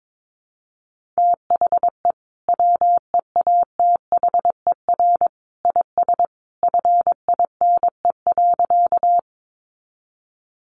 Morse code, heard as a keyed tone, about 22 wpm, 700 Hz. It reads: THE WEATHER IS FINE.
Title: Sentences from Top 800 Words 22wpm